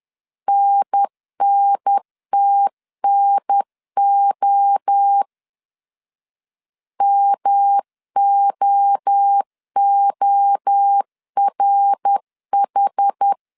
Morse code, heard as a keyed tone, about 10 wpm, 800 Hz. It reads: NNTNO MOORH
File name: 10wpm